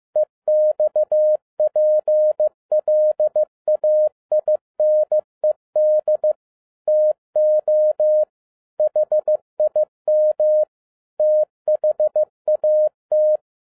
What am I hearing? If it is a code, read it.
Morse code, heard as a keyed tone, about 15 wpm, 610 Hz. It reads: EXPLAINED TO HIM THAT
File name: 2011-10-mp3 / n07-A DOT 15wpm.mp3